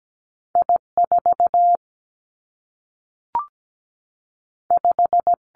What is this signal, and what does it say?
Morse code, heard as a keyed tone, about 17 wpm, 700 Hz. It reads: I 4 5
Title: Two Challenging Letter-Number Combinations 17wpm